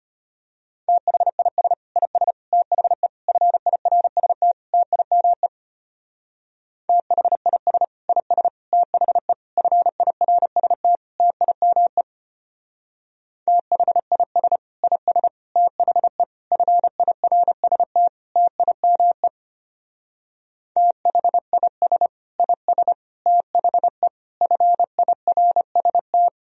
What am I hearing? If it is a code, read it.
Morse code, heard as a keyed tone, about 30 wpm, 700 Hz. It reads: THIS IS THE FIRST TIME THIS IS THE FIRST TIME THIS IS THE FIRST TIME THIS IS THE FIRST
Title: Speed-Racing - Sets of 5 Words 25wpm